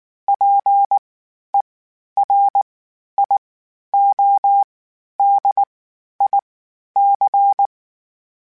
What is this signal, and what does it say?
Morse code, heard as a keyed tone, about 19 wpm, spaced as for 11 wpm, 800 Hz. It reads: PERIODIC
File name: (spaced) / FromSMs spaced mp3s / Antennas sp@18